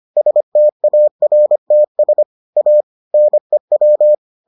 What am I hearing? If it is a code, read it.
Morse code, heard as a keyed tone, about 25 wpm, 600 Hz. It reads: STARTS A NEW